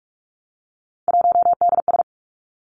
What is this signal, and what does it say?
Morse code, heard as a keyed tone, about 46 wpm, 700 Hz. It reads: 1DS
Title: Three Challenging Letter-Number Combinations 45wpm